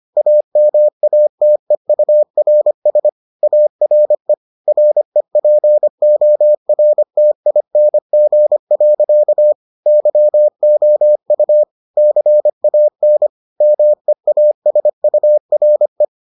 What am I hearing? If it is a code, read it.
Morse code, heard as a keyed tone, about 25 wpm, 600 Hz. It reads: AMATEURS ARE REPORTING. YOU CAN MEASURE